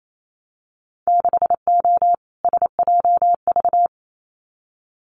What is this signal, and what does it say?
Morse code, heard as a keyed tone, about 28 wpm, 700 Hz. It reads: 6O SJV
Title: Winter Field Day Exchanges Encore 28wpm